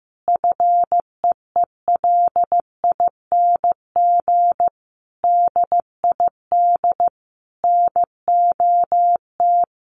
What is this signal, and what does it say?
Morse code, heard as a keyed tone, about 15 wpm, 700 Hz. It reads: FEELING DID NOT
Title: Sentences from Top 1200 Words 15wpm